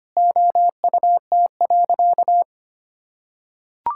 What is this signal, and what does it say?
Morse code, heard as a keyed tone, about 25 wpm, 700 Hz. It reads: OUT.
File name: practice / 25wpm / Sentences with 3-Word Phrases - 25wpm